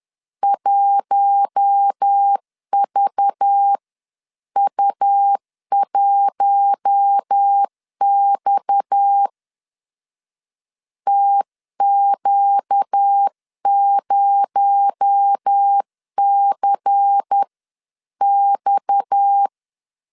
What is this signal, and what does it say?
Morse code, heard as a keyed tone, about 10 wpm, spaced as for 9 wpm, 800 Hz. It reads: 1VU1X TQ0CX
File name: CW / DET / 1 6 1500C S C 10wpm 4252